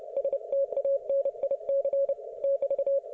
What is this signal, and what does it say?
Morse code, heard as a keyed tone, about 28 wpm, 550 Hz. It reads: STANIC <BT>
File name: Archiv / CUC181 / ok1fhi cul181